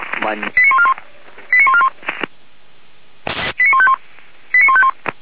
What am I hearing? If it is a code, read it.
Five-tone selective call (ZVEI), heard as quick runs of five tones, tones of about 70 ms, with speech. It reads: 89141 89141 89151 89151